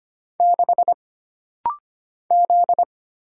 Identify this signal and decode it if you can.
Morse code, heard as a keyed tone, about 25 wpm, 700 Hz. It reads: 6 Z
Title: Single Letter-Number - Rapid-Fire 25wpm